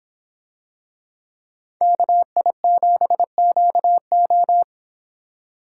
Morse code, heard as a keyed tone, about 26 wpm, 700 Hz. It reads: KI7QO